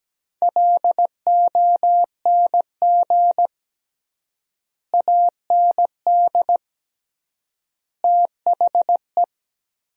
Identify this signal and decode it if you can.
Morse code, heard as a keyed tone, about 17 wpm, 700 Hz. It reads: LONG AND THE
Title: 3x Word-Spacing - Sentences from Top 400 Words - 17wpm